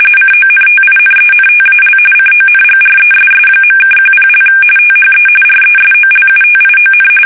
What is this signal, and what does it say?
RTTY, about 45.45 baud. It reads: CONTINENTAL CONGRESS.(10) THE RESOLUTION
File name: fsk850 / usnavy1